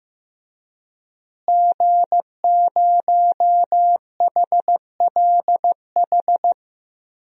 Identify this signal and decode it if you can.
Morse code, heard as a keyed tone, about 15 wpm, 700 Hz. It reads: G0HLH
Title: DXCC-100 Call Signs - 5 Characters - Repeated 3x - 15wpm